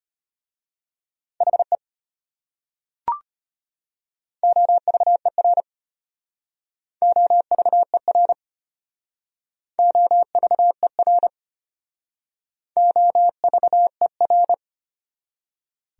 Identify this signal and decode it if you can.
Morse code, heard as a keyed tone, about 33 wpm, 700 Hz. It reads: HE OVER OVER OVER OVER